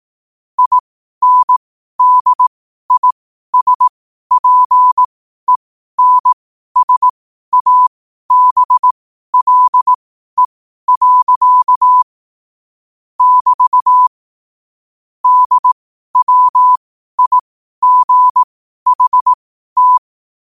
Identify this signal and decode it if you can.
Morse code, heard as a keyed tone, about 18 wpm, spaced as for 12 wpm, 1000 Hz. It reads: INDISPENSABLE. <BT> DWIGHT
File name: qotdfiles / 12wpm